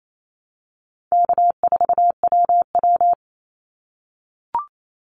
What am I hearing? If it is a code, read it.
Morse code, heard as a keyed tone, about 28 wpm, 700 Hz. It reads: K4WW
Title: Call Signs - 28wpm